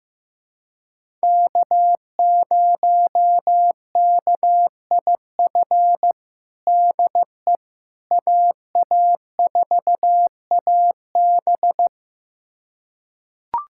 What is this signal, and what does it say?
Morse code, heard as a keyed tone, about 15 wpm, 700 Hz. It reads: K0KIF DE AA4AB
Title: USA Callsign DE Callsign - 5 Characters - 15wpm